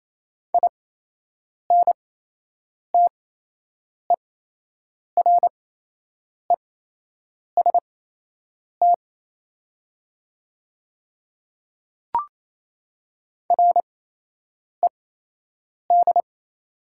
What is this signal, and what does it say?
Morse code, heard as a keyed tone, about 28 wpm, spaced as for 8 wpm, 690 Hz. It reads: INTEREST RED
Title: 8x Character Spacing - Top 500 Words - 28wpm